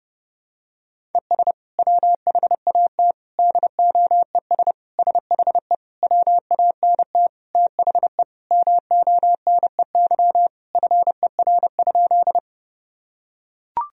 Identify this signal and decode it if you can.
Morse code, heard as a keyed tone, about 32 wpm, 700 Hz. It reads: ES WHAT DOES SHE WANT THE MONEY FER?
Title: Sentences from Top 400 Words and Abbreviations 30wpm